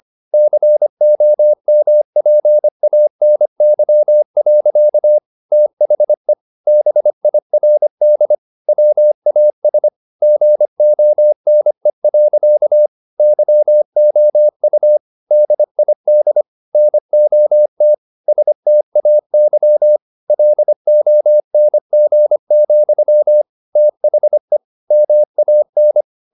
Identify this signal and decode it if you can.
Morse code, heard as a keyed tone, about 25 wpm, 600 Hz. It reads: COMPANY. THE BIRD WAS GONE. YOU DID NOT STAY LONG, THE MAN
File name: the old man and the sea part 1 25wpm